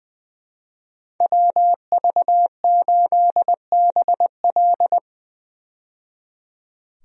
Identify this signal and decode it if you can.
Morse code, heard as a keyed tone, about 20 wpm, 700 Hz. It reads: WV8BL
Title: USA Call Signs - 5 Characters - 20wpm